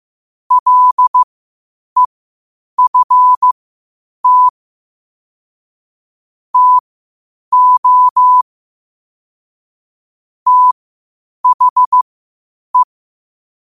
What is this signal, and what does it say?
Morse code, heard as a keyed tone, about 15 wpm, spaced as for 8 wpm, 1000 Hz. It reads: LEFT TO THE
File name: qotdfiles / 8wpm